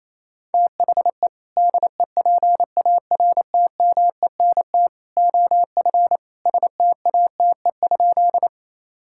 Morse code, heard as a keyed tone, about 28 wpm, 700 Hz. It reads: THE DEPARTMENT OF STATE?
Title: Sentences from Top 1000 Words - Review Part 5 - 28wpm